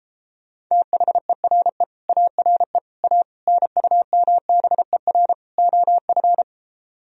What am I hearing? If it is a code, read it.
Morse code, heard as a keyed tone, about 33 wpm, 700 Hz. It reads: THERE ARE A NUMBER OF